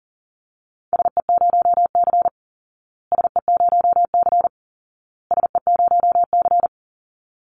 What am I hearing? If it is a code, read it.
Morse code, heard as a keyed tone, about 40 wpm, 700 Hz. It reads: SE0C SE0C SE0C
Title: Call Signs Repeated 2x - 40wpm